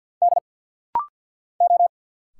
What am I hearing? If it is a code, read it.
Morse code, heard as a keyed tone, about 50 wpm, 700 Hz. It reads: D O
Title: Single Letters - Rapid-Fire 50wpm